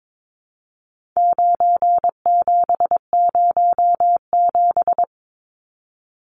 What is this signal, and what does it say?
Morse code, heard as a keyed tone, about 22 wpm, 700 Hz. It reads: 9707
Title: Numbers - 4-Digit 22wpm